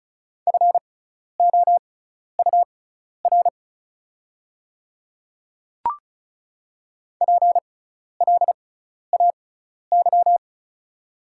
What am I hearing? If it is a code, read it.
Morse code, heard as a keyed tone, about 34 wpm, spaced as for 12 wpm, 690 Hz. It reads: FOUR PLAY
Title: 6x Character Spacing - Top 300 Words - 35wpm